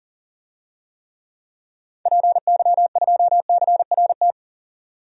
Morse code, heard as a keyed tone, about 40 wpm, 700 Hz. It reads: PY2CRT
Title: DXCC-100 Call Signs - 6 Characters - 40wpm